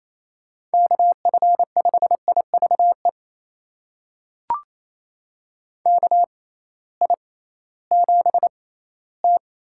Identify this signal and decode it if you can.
Morse code, heard as a keyed tone, about 28 wpm, spaced as for 16 wpm, 700 Hz. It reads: KF5IVE KI7T
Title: USA Call Signs - 6 Characters - 6x Character Spacing - 28wpm